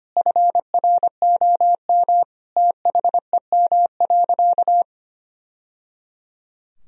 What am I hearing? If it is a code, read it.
Morse code, heard as a keyed tone, about 25 wpm, 700 Hz. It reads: FROM THEM.
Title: Sentences from Top 1000 Words - Review Part 4 - 25wpm